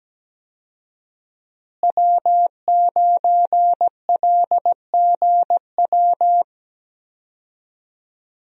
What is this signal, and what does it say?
Morse code, heard as a keyed tone, about 17 wpm, 700 Hz. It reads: W9LGW
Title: USA Call Signs - 5 Characters - Repeated 2x - 17wpm